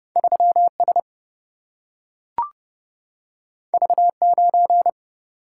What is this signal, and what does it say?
Morse code, heard as a keyed tone, about 30 wpm, 700 Hz. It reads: 3 S V 9
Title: Single-Letter Single-Number 30wpm